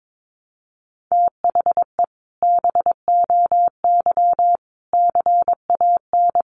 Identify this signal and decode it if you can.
Morse code, heard as a keyed tone, about 22 wpm, 700 Hz. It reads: THE BOY CAN
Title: Additional Easy Phrases 22wpm